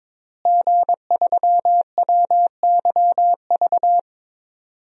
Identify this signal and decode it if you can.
Morse code, heard as a keyed tone, about 22 wpm, 700 Hz. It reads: G3WYV
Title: DXCC-100 Call Signs - 5 Characters - Repeated 3x - 22wpm